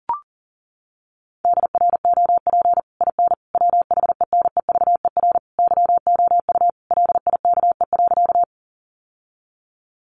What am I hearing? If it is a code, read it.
Morse code, heard as a keyed tone, about 37 wpm, 700 Hz. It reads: DROP IN WHENEVER YOU LIKE.
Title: Sentences from Top 1400 Words 40wpm